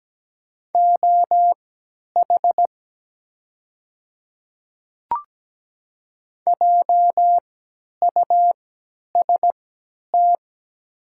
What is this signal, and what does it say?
Morse code, heard as a keyed tone, about 17 wpm, spaced as for 10 wpm, 700 Hz. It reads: OH JUST
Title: 3x Character Spacing - Top 500 Words - 17wpm